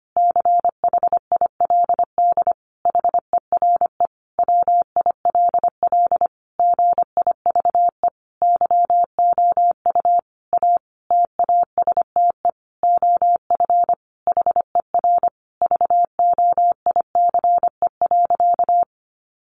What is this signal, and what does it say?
Morse code, heard as a keyed tone, about 25 wpm, 700 Hz. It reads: CHILD HERE WILL GIVE YOU A TASTE OF HER VOICE.
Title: Sentences from Top 1100 Words 25wpm